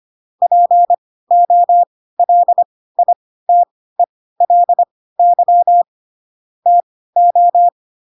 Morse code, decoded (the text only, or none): POLITELY TO